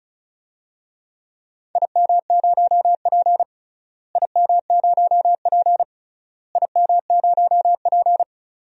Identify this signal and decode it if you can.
Morse code, heard as a keyed tone, about 36 wpm, 700 Hz. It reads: IM0P IM0P IM0P
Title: DXCC-100 Call Signs - 4 Characters - Repeated 3x - 35wpm